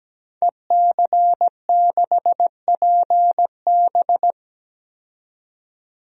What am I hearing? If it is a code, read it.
Morse code, heard as a keyed tone, about 17 wpm, 700 Hz. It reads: EC6PB
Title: DXCC-100 Call Signs - 5 Characters - 17wpm